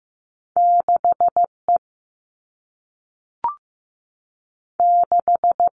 Morse code, heard as a keyed tone, about 15 wpm, 700 Hz. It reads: 6 E 6
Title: All Cut Numbers - 2-Digit 15wpm